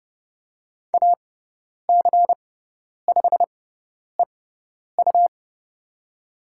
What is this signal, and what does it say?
Morse code, heard as a keyed tone, about 30 wpm, spaced as for 10 wpm, 700 Hz. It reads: AC5EU
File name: Lesson 17g - Letter u - Call Signs 30wpm